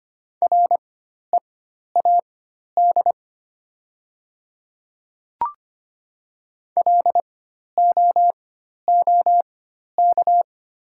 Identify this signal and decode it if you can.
Morse code, heard as a keyed tone, about 25 wpm, spaced as for 12 wpm, 700 Hz. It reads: READ LOOK
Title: 4x Character Spacing - Top 200 Words - 25wpm